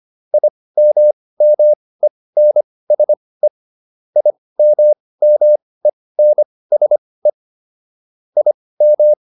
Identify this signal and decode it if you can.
Morse code, heard as a keyed tone, about 25 wpm, spaced as for 18 wpm, 600 Hz. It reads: IMMENSE IMMENSE IM